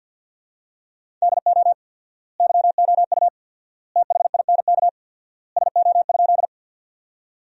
Morse code, heard as a keyed tone, about 50 wpm, 700 Hz. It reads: DO YOU THINK SO?
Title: 4x Word-Spacing - Sentences from Top 200 Words - 50wpm